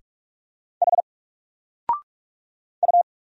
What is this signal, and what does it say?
Morse code, heard as a keyed tone, about 46 wpm, 700 Hz. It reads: H U
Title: Single Letter-Number 45wpm